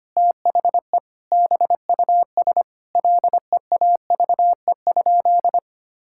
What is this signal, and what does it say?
Morse code, heard as a keyed tone, about 25 wpm, 700 Hz. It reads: THE BUS LEAVE?
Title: Sentences from Top 1000 Words - Review Part 6 - 25wpm